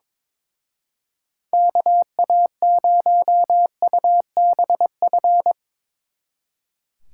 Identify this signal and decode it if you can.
Morse code, heard as a keyed tone, about 24 wpm, 700 Hz. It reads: KA0UBF
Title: Speed Racing - USA 2x3 Call Signs 22wpm